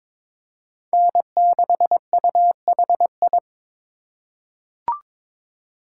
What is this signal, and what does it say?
Morse code, heard as a keyed tone, about 22 wpm, 700 Hz. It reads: N6UHI
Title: USA Call Signs - 5 Characters - 4x Character Spacing - 22wpm